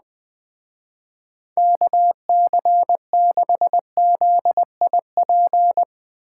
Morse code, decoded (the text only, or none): KC6ZIP